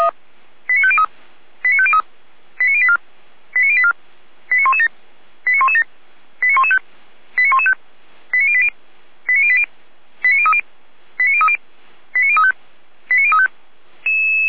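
Five-tone selective call (ZVEI), heard as quick runs of five tones, tones of about 70 ms. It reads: 89503 89503 89084 89084 89117 89117 89115 89115 89080 89080 89030 89030 89035 89035
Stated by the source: ZVEI five-tone selective call received on SDR.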